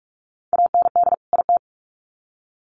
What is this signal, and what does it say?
Morse code, heard as a keyed tone, about 44 wpm, 700 Hz. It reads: AND IT
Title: Sets of 2 Words 45wpm